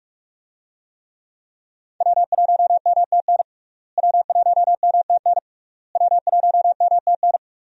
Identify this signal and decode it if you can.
Morse code, heard as a keyed tone, about 47 wpm, 700 Hz. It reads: W1MTN W1MTN W1MTN
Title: USA Call Signs - 5 Characters - Repeated 3x - 45wpm